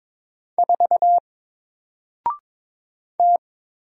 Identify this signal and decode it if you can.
Morse code, heard as a keyed tone, about 22 wpm, 700 Hz. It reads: 4 T
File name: Single-Letter-Number-22wpm